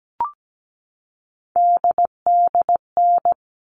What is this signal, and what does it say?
Morse code, heard as a keyed tone, about 17 wpm, 700 Hz. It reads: DDN
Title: All Cut Numbers - 3-Digit 17wpm